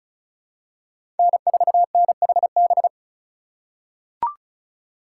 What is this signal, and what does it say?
Morse code, heard as a keyed tone, about 36 wpm, 700 Hz. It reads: N4NHB
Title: USA Call Signs - 5 Characters - 2x Character Spacing - 35wpm